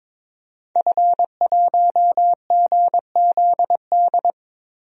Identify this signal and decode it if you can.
Morse code, heard as a keyed tone, about 22 wpm, 700 Hz. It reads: F1GZD
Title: DXCC-100 Call Signs - 5 Characters - Repeated 2x - 22wpm